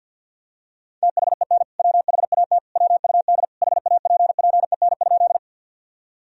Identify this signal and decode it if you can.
Morse code, heard as a keyed tone, about 51 wpm, 700 Hz. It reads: THEN WHAT WUD HAPPEN?
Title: Sentences from Top 400 Words and Abbreviations 50wpm